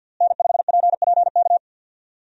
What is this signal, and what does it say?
Morse code, heard as a keyed tone, about 52 wpm, 700 Hz. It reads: N5PPK
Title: USA Call Signs - 5 Characters - Repeated 3x - 50wpm